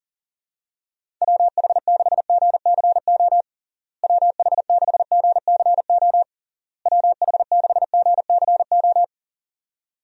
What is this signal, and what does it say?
Morse code, heard as a keyed tone, about 40 wpm, 700 Hz. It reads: WH6GCO WH6GCO WH6GCO
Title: USA Call Signs - 6 Characters - Repeated 3x - 40wpm